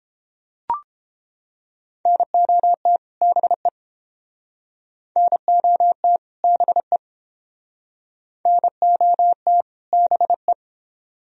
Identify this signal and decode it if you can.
Morse code, heard as a keyed tone, about 31 wpm, 700 Hz. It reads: NOT BE NOT BE NOT BE